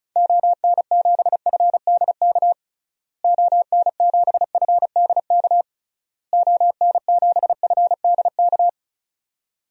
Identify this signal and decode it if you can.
Morse code, heard as a keyed tone, about 36 wpm, 700 Hz. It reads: ON7FDK ON7FDK ON7FDK
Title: DXCC-100 Call Signs - 6 Characters - Repeated 3x - 35wpm